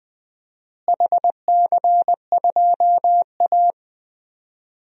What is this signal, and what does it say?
Morse code, heard as a keyed tone, about 19 wpm, 700 Hz. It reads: HC2A
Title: DXCC-100 Call Signs - 4 Characters - Repeated 3x - 20wpm